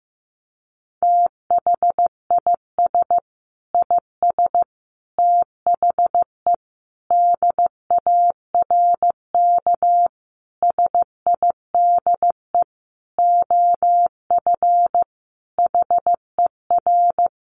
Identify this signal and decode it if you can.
Morse code, heard as a keyed tone, about 15 wpm, 700 Hz. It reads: THIS IS THE DARK SIDE OF HER
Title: Sentences from Top 1200 Words 15wpm